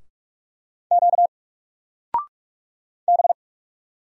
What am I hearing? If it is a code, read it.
Morse code, heard as a keyed tone, about 44 wpm, 700 Hz. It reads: Q B
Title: Single Letter-Number 45wpm